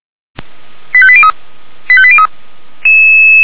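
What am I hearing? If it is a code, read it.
Five-tone selective call (ZVEI), heard as quick runs of five tones, tones of about 70 ms. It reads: 85803 85803
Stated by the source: ZVEI five-tone call 85749 received on an SDR, logged by feuerwehr-orlamuende.org fire brigade.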